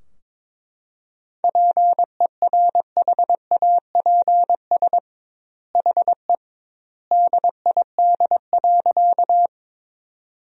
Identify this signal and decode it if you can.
Morse code, heard as a keyed tone, about 22 wpm, 700 Hz. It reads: PERHAPS HE DID.
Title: 2x Word-Spacing - Sentences from Top 500 Words - 22wpm